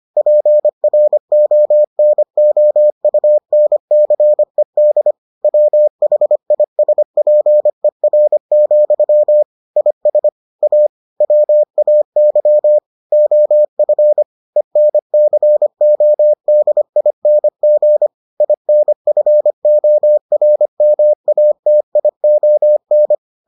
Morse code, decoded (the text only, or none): PRONOUNCED WHISPER, IS A WAY OF ENCODING INFORMATION